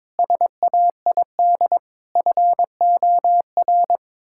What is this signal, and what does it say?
Morse code, heard as a keyed tone, about 22 wpm, 700 Hz. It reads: SAID FOR
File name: practice / 22wpm / Sets of 7 Words - 22wpm